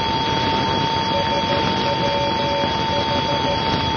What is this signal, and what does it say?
Morse code, heard as a keyed tone, about 13 wpm, 570 Hz, with noise. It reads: SWH
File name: SWH NDB 1988